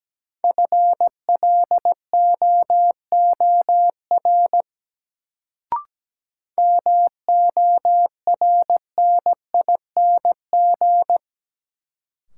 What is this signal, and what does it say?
Morse code, heard as a keyed tone, about 17 wpm, 700 Hz. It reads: FLOOR MORNING
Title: Sets of 3 Words - New Word Drill - 17wpm